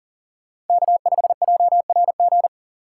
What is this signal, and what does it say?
Morse code, heard as a keyed tone, about 52 wpm, 700 Hz. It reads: K5JRG